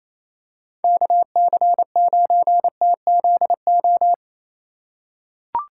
Morse code, heard as a keyed tone, about 27 wpm, 700 Hz. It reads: KC9TZO
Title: K1USN SST QSO Practice - 28wpm